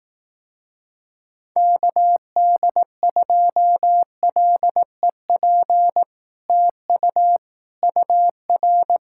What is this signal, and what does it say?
Morse code, heard as a keyed tone, about 18 wpm, 700 Hz. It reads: KD2LEP TU UR
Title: Speed-Racing Common QSO Phrases Encore - 15wpm